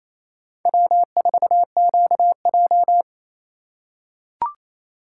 Morse code, heard as a keyed tone, about 28 wpm, 700 Hz. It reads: W4QJ
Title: USA Call Signs - 4x Character Spacing - 28wpm